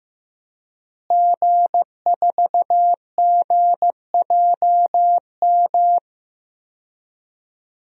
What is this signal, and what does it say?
Morse code, heard as a keyed tone, about 15 wpm, 700 Hz. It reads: G4GJM
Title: DXCC-100 Call Signs - 5 Characters - Repeated 2x - 15wpm